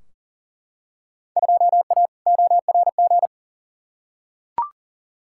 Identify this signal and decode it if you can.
Morse code, heard as a keyed tone, about 40 wpm, 700 Hz. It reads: 2A ORG
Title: Field Day Exchanges Encore - 40wpm